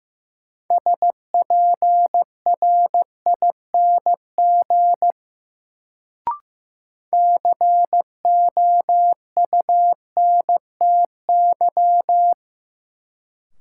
Morse code, heard as a keyed tone, about 15 wpm, 700 Hz. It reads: SPRING COUNTY
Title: Sentences from Top 1200 Words - New Word Drill - 15wpm